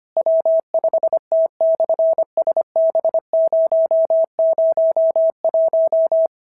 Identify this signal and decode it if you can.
Morse code, heard as a keyed tone, about 25 wpm, 690 Hz. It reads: W5T/SB001
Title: SOTA QSO Practice 25wpm